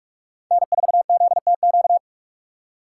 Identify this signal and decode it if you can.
Morse code, heard as a keyed tone, about 45 wpm, 700 Hz. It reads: N4ZTQ